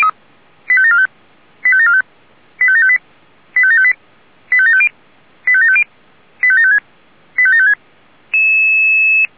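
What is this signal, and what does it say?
Five-tone selective call (ZVEI), heard as quick runs of five tones, tones of about 70 ms. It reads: 85745 85745 85758 85758 85750 85750 85756 85756
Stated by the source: ZVEI five-tone selective call, address 85749, received on SDR by Feuerwehr Orlamünde.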